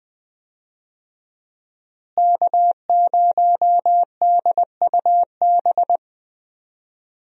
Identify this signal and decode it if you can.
Morse code, heard as a keyed tone, about 20 wpm, 700 Hz. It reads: K0DUB
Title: USA Call Signs - 5 Characters - Repeated 2x - 20wpm